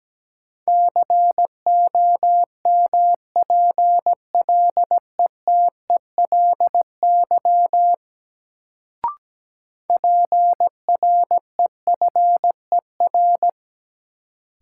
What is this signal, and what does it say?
Morse code, heard as a keyed tone, about 17 wpm, 700 Hz. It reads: COMPLETELY PREFER
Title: Sentences from Top 1100 Words - New Word Drill Alt - 17wpm